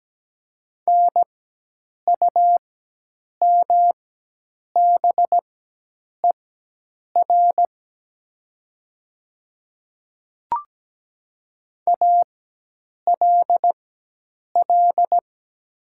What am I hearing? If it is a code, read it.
Morse code, heard as a keyed tone, about 17 wpm, spaced as for 8 wpm, 700 Hz. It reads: NUMBER ALL
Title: 4x Character Spacing - Top 100 Words - 17wpm